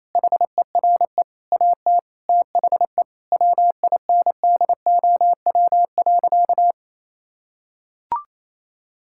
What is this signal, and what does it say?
Morse code, heard as a keyed tone, about 28 wpm, 700 Hz. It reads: HERE AT THE WINDOW.
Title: Sentences from Top 1600 Words 28wpm